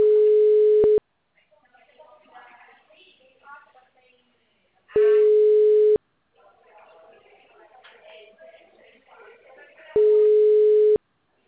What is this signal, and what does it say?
Ringback tone (Europe, 425 Hz), with speech.